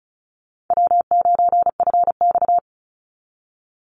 Morse code, heard as a keyed tone, about 34 wpm, 700 Hz. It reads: W9FX
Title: Call Signs Repeated 3x - 35wpm